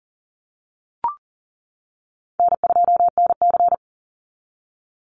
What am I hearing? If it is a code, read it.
Morse code, heard as a keyed tone, about 40 wpm, 700 Hz. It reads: N2NC
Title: Call Signs - 40wpm